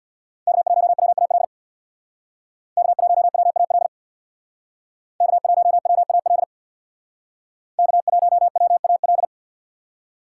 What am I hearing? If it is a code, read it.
Morse code, heard as a keyed tone, about 62 wpm, 700 Hz. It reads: K1WAL K1WAL K1WAL K1WAL
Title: Speed Racing - USA Call Signs 50wpm